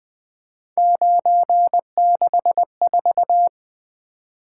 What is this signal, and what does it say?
Morse code, heard as a keyed tone, about 20 wpm, 700 Hz. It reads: 964
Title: Three Letter-Number Combinations - 20wpm